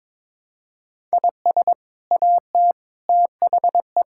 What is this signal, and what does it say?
Morse code, heard as a keyed tone, about 21 wpm, 700 Hz. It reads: IS AT THE
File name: practice / 22wpm / Sets of 6 Words - 22wpm